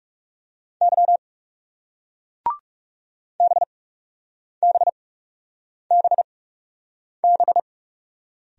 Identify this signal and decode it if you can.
Morse code, heard as a keyed tone, about 37 wpm, spaced as for 8 wpm, 700 Hz. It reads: Y BBBB
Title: Speed-Racing Single Letters 30wpm